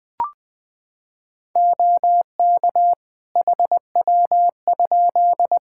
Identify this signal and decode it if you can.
Morse code, heard as a keyed tone, about 20 wpm, 700 Hz. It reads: OK HW?
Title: Practice QSO - Exchange 2 - Info - 20wpm